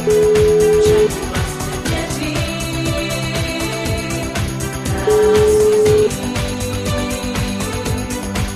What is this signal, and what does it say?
Ringback tone (Europe, 425 Hz), with noise.